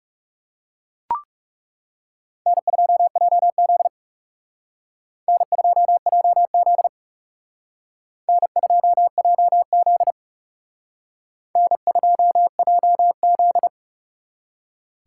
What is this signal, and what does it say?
Morse code, heard as a keyed tone, about 34 wpm, 700 Hz. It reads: N2JZ N2JZ N2JZ N2JZ
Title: Speed Racing - USA 1x2 Call Signs 30wpm